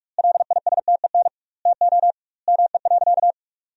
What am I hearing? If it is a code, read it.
Morse code, heard as a keyed tone, about 46 wpm, 700 Hz. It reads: LISTEN TO ME.
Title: 2x Word-Spacing - Sentences from Top 400 Words - 45wpm